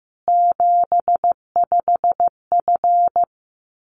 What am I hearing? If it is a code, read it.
Morse code, heard as a keyed tone, about 15 wpm, 700 Hz. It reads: 75F
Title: Common QSO Phrases 15wpm